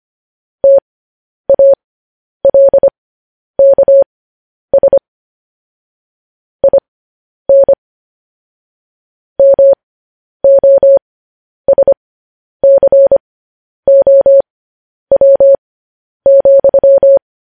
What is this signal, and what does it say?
Morse code, heard as a keyed tone, about 25 wpm, spaced as for 10 wpm, 550 Hz. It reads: TALKS IN MOSCOW,
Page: News Headlines in Morse Code at 10 WPM